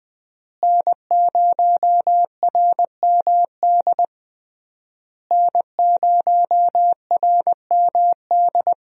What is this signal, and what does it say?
Morse code, heard as a keyed tone, about 21 wpm, 700 Hz. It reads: N0RMD N0RMD
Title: USA Call Signs - 5 Characters - Repeated 3x - 20wpm